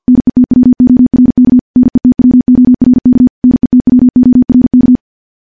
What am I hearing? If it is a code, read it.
Morse code, heard as a keyed tone, about 50 wpm, 250 Hz. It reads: NETWORK NETWORK NETWORK
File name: morse / morse-training-50WPM